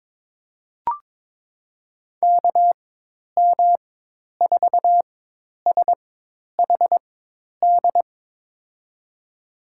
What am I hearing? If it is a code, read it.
Morse code, heard as a keyed tone, about 21 wpm, spaced as for 10 wpm, 700 Hz. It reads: KM4SHD
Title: USA Call Signs - 6 Characters - 4x Character Spacing - 22wpm